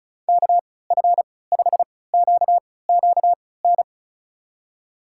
Morse code, heard as a keyed tone, about 36 wpm, spaced as for 20 wpm, 700 Hz. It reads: KF5QQN
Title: USA Call Signs - 6 Characters - 3x Character Spacing - 35wpm